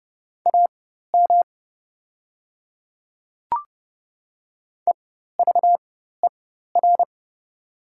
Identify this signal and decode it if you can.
Morse code, heard as a keyed tone, about 29 wpm, spaced as for 14 wpm, 690 Hz. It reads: AM EVER
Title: 4x Character Spacing - Top 500 Words - 30wpm